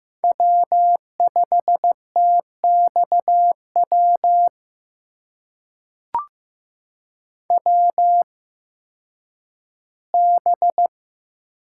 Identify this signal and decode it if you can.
Morse code, heard as a keyed tone, about 15 wpm, spaced as for 10 wpm, 700 Hz. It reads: W5TXW WB
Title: USA Call Signs - 8x Character Spacing - 15wpm